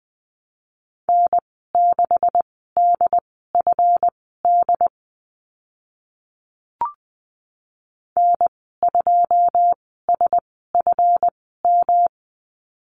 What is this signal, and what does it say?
Morse code, heard as a keyed tone, about 20 wpm, spaced as for 14 wpm, 700 Hz. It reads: N6DFD N2SFM
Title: 140 Lesson 23g - Letter f - Call Signs 20wpm